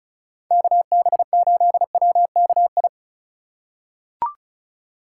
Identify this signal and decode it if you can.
Morse code, heard as a keyed tone, about 36 wpm, 700 Hz. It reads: KB8WKI
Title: USA Call Signs - 6 Characters - 4x Character Spacing - 35wpm